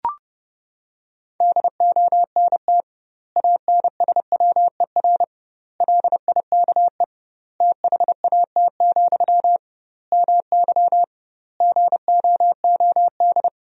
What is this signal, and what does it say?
Morse code, heard as a keyed tone, about 30 wpm, 700 Hz. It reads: DONT ANSWER LIKE THAT, MY GOOD
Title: 2x Word-Spacing - Sentences from Top 400 Words - 30wpm